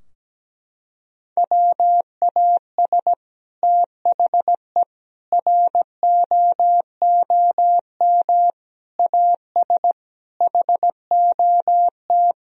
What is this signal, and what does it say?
Morse code, heard as a keyed tone, about 17 wpm, 700 Hz. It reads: WAS THE ROOM AS HOT
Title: Sentences from Top 1000 Words - Review Part 4 - 17wpm